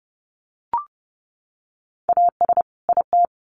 Morse code, heard as a keyed tone, about 30 wpm, 700 Hz. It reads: AS IT
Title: Sets of 2 Words 30wpm